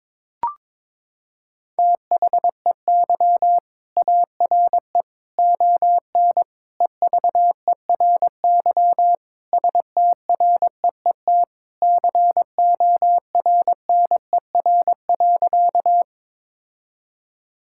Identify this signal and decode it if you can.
Morse code, heard as a keyed tone, about 22 wpm, 700 Hz. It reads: THEY ARE ON EVERY STREET CORNER.
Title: Sentences from Top 1600 Words 22wpm